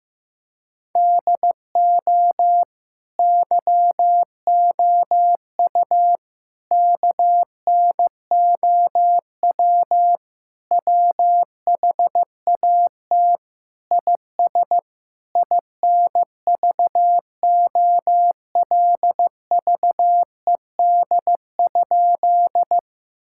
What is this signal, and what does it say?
Morse code, heard as a keyed tone, about 15 wpm, 700 Hz. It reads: DO YOU KNOW WHAT IS INVOLVED?
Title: Sentences from Top 1000 Words - Review Part 5 - 15wpm